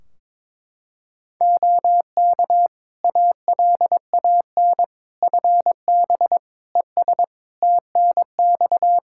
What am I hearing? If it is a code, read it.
Morse code, heard as a keyed tone, about 22 wpm, 700 Hz. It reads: OK ALAN FB ES TNX
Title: Practice QSO - Exchange 2 - Info - 22wpm